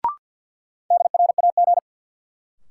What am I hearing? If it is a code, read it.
Morse code, heard as a keyed tone, about 53 wpm, 700 Hz. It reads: DRAG
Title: Sentences from Top 1500 Words - New Word Drill - 50wpm